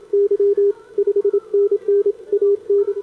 Morse code, heard as a keyed tone, about 25 wpm, 410 Hz, with radio noise. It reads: Y 5NN AN
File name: CQWWCW2025 / 20M SCAN CN3A